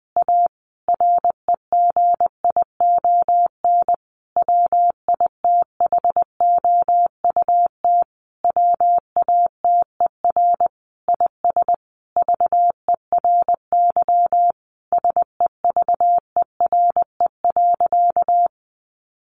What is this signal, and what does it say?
Morse code, heard as a keyed tone, about 20 wpm, 700 Hz. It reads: A REGION WITHOUT WATER IS VERY SEVERE.
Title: Sentences from Top 1500 Words 20wpm